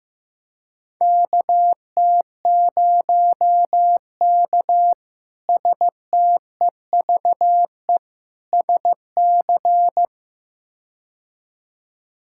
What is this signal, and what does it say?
Morse code, heard as a keyed tone, about 15 wpm, 700 Hz. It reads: KT0K STEVE SC
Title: K1USN SST QSO Practice - 15wpm